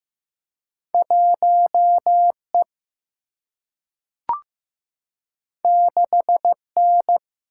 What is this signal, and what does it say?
Morse code, heard as a keyed tone, about 15 wpm, 700 Hz. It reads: 1E 6N
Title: Single-Letter Single-Number 15wpm